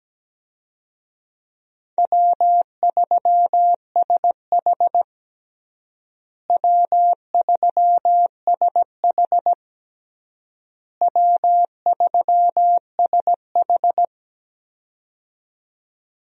Morse code, heard as a keyed tone, about 17 wpm, 700 Hz. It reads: W3SH W3SH W3SH
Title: USA Call Signs - 4 Characters - Repeated 3x - 17wpm